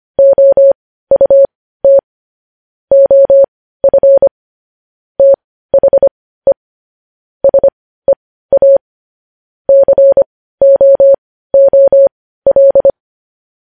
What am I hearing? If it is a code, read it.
Morse code, heard as a keyed tone, about 25 wpm, spaced as for 15 wpm, 550 Hz. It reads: OUT OF THE SEA COOL
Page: news headlines in morse code at 15 wpm